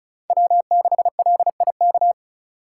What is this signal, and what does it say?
Morse code, heard as a keyed tone, about 36 wpm, 700 Hz. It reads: W6LIK
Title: USA Call Signs - 5 Characters - 3x Character Spacing - 35wpm